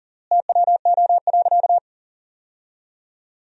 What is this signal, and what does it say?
Morse code, heard as a keyed tone, about 40 wpm, 700 Hz. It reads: TWO.
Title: 3x Word-Spacing - Sentences from Top 300 Words - 40wpm